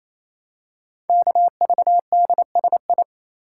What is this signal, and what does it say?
Morse code, heard as a keyed tone, about 27 wpm, 700 Hz. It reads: KVDSI
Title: Code Groups - 5 Letters - 28wpm